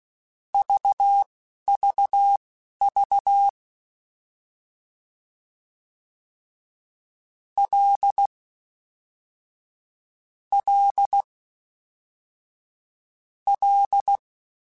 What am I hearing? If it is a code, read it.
Morse code, heard as a keyed tone, about 16 wpm, spaced as for 11 wpm, 800 Hz. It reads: VVV L L L
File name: F and L / F and L mp3 / F and L Prefixes 3x2 Twice @10